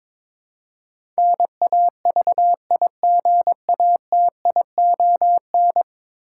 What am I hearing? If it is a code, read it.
Morse code, heard as a keyed tone, about 23 wpm, 700 Hz. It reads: NAVIGATION
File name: practice / 22wpm / 10 Letter Words - First 3 Letters Given - 22wpm